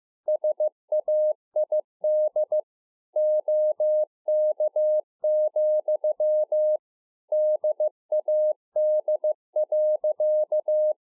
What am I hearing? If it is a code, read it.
Morse code, heard as a keyed tone, about 15 wpm, 610 Hz. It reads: SAID OK, DAD.